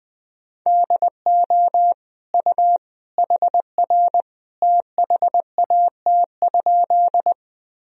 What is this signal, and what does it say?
Morse code, heard as a keyed tone, about 20 wpm, 700 Hz. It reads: DO U HR THAT?
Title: Sentences from Top 300 Words and Abbreviations 20wpm